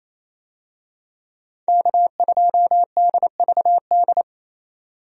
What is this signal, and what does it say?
Morse code, heard as a keyed tone, about 28 wpm, 700 Hz. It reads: K2DVD
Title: USA Call Signs - 5 Characters - 28wpm